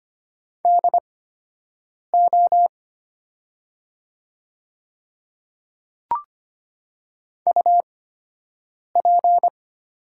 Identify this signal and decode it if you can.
Morse code, heard as a keyed tone, about 25 wpm, spaced as for 7 wpm, 700 Hz. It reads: DO UP
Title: 8x Character Spacing - Top 100 Words - 25wpm